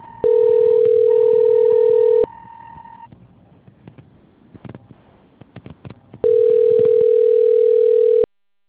Ringback tone (North America), 440 + 480 Hz.